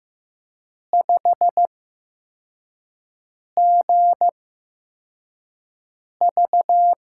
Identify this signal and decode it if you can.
Morse code, heard as a keyed tone, about 15 wpm, spaced as for 4 wpm, 700 Hz. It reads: 5GV
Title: USA Call Signs - 4 Characters - 8x Character Spacing - 15wpm